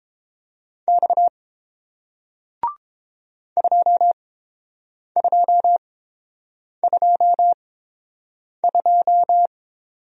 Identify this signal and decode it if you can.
Morse code, heard as a keyed tone, about 30 wpm, spaced as for 7 wpm, 700 Hz. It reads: X 2222